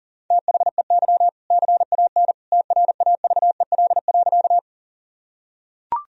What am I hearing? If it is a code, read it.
Morse code, heard as a keyed tone, about 40 wpm, 700 Hz. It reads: THEY CAN TRAVEL.
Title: Sentences from Top 1000 Words - Review Part 2 - 40wpm